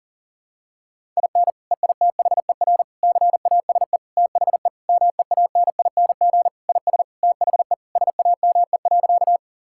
Morse code, heard as a keyed tone, about 40 wpm, 700 Hz. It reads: IN EITHER CASE THE MEANING IS THE SAME.
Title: Sentences from Top 1000 Words - Review Part 6 - 40wpm